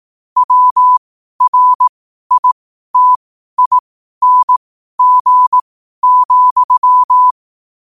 Morse code, heard as a keyed tone, about 18 wpm, spaced as for 12 wpm, 1000 Hz. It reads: WRITING,
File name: qotdfiles / 12wpm